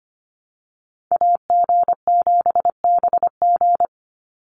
Morse code, heard as a keyed tone, about 25 wpm, 700 Hz. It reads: AG7BG
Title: USA Call Signs 25wpm